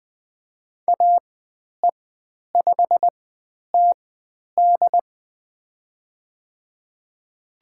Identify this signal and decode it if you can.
Morse code, heard as a keyed tone, about 20 wpm, spaced as for 10 wpm, 700 Hz. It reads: AE5TD
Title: 102 Lesson 17g - Letter u - Call Signs 20wpm